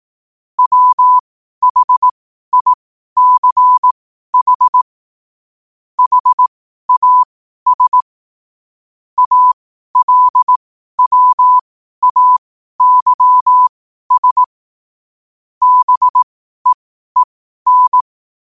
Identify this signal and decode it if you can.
Morse code, heard as a keyed tone, about 18 wpm, spaced as for 12 wpm, 1000 Hz. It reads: WHICH HAS ALWAYS BEEN
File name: qotdfiles / 12wpm